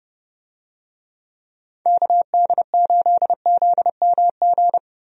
Morse code, heard as a keyed tone, about 30 wpm, 700 Hz. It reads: KD8ZMG
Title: USA Call Signs - 6 Characters - Repeated 2x - 30wpm